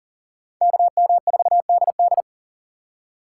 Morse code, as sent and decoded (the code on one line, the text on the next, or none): -.- -- ....- -.. -..
KM4DD